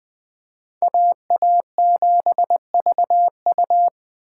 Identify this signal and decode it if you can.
Morse code, heard as a keyed tone, about 20 wpm, 700 Hz. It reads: AA7VU